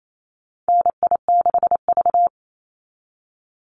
Morse code, heard as a keyed tone, about 28 wpm, 700 Hz. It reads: NI6V
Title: Call Signs - 28wpm